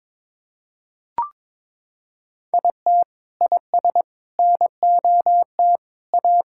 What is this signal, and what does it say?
Morse code, heard as a keyed tone, about 22 wpm, 700 Hz. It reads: IT IS NOT A
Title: Sentences from Top 1600 Words 22wpm